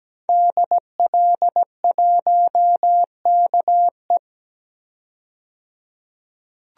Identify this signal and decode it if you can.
Morse code, heard as a keyed tone, about 17 wpm, 700 Hz. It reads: DL1KE
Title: DXCC-100 Call Signs - 5 Characters - 17wpm